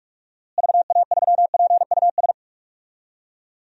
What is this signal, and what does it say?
Morse code, heard as a keyed tone, about 44 wpm, 700 Hz. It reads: VA3PUS
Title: DXCC-100 Call Signs - 6 Characters - Repeated 2x - 45wpm